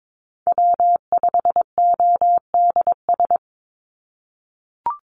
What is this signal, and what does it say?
Morse code, heard as a keyed tone, about 22 wpm, 700 Hz. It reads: W5ODS
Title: NA SOTA Activator Callsigns 22wpm